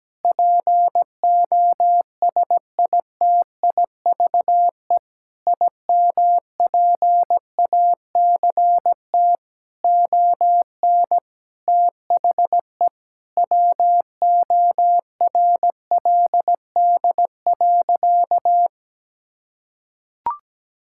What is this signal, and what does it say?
Morse code, heard as a keyed tone, about 17 wpm, 700 Hz. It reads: POSITIVE IMPACT ON THE WORLD.
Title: Sentences from Top 1000 Words - Review Part 6 - 17wpm